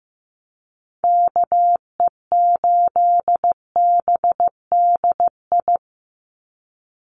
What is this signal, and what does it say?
Morse code, heard as a keyed tone, about 15 wpm, 700 Hz. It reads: KE8BDI
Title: USA Call Signs 15wpm